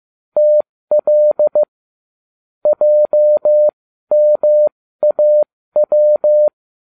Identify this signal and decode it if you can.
Morse code, heard as a keyed tone, about 15 wpm, spaced as for 13 wpm, 600 Hz. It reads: TL JMAW